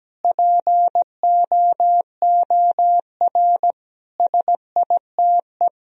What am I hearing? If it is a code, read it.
Morse code, heard as a keyed tone, about 17 wpm, 700 Hz. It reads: POOR SITE
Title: Sentences from Top 1000 Words - Review Part 4 - 17wpm